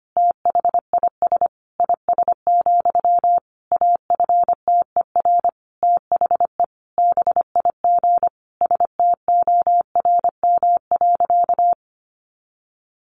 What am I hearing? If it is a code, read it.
Morse code, heard as a keyed tone, about 25 wpm, 700 Hz. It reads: THIS IS, AFTER THE BIG STORM.
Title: Sentences from Top 1500 Words 25wpm